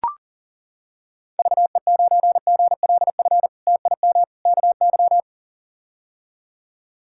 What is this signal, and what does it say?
Morse code, heard as a keyed tone, about 38 wpm, 700 Hz. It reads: VE9GLF TIM KY
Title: K1USN SST QSO Practice - 40wpm